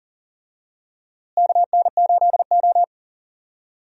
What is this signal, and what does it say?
Morse code, heard as a keyed tone, about 40 wpm, 700 Hz. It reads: KN8O
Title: USA Call Signs - 4 Characters - Repeated 2x - 40wpm